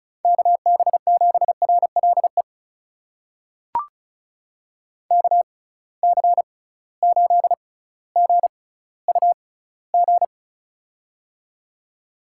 Morse code, heard as a keyed tone, about 36 wpm, spaced as for 18 wpm, 700 Hz. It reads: KB7RLE KC8GUG
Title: USA Call Signs - 6 Characters - 6x Character Spacing - 35wpm